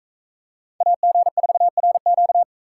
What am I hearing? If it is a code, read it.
Morse code, heard as a keyed tone, about 41 wpm, 700 Hz. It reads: AG4RQ